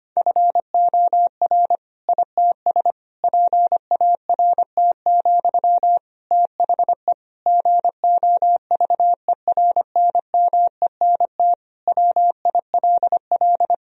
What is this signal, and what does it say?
Morse code, heard as a keyed tone, about 25 wpm, 700 Hz. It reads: FOR ITS PART, THE GOVERNMENT WILL
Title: Sentences from Top 1000 Words - Review Part 3 - 25wpm